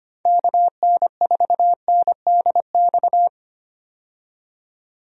Morse code, heard as a keyed tone, about 25 wpm, 700 Hz. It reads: KN4NDX
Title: USA Call Signs - 6 Characters - 25wpm